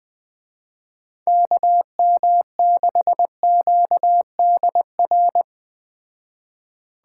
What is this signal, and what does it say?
Morse code, heard as a keyed tone, about 20 wpm, 700 Hz. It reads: KM6QDR